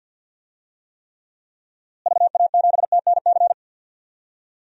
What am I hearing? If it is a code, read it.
Morse code, heard as a keyed tone, about 51 wpm, 700 Hz. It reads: VA7TNC
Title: DXCC-100 Call Signs - 6 Characters - Repeated 3x - 50wpm